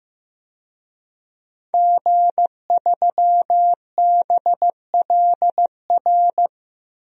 Morse code, heard as a keyed tone, about 16 wpm, 700 Hz. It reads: G3BLR
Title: DXCC-100 Call Signs - 5 Characters - Repeated 3x - 15wpm